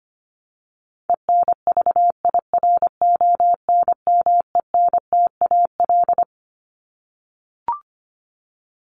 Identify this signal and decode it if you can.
Morse code, heard as a keyed tone, about 25 wpm, 700 Hz. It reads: ENVIRONMENTAL
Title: Top 1001-1250 Words 25wpm